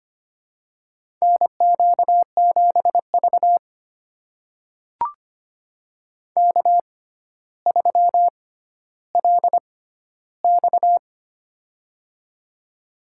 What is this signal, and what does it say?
Morse code, heard as a keyed tone, about 25 wpm, spaced as for 13 wpm, 700 Hz. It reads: NQ7V K3LX
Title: USA Call Signs - 4 Characters - 6x Character Spacing - 25wpm